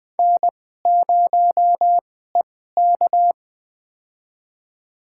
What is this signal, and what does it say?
Morse code, heard as a keyed tone, about 20 wpm, spaced as for 14 wpm, 700 Hz. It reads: N0EK
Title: USA Call Signs - 4 Characters - 2x Character Spacing - 20wpm